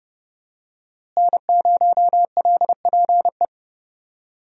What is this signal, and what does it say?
Morse code, heard as a keyed tone, about 30 wpm, 700 Hz. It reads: N0LPE
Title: USA Call Signs - 5 Characters - Repeated 2x - 30wpm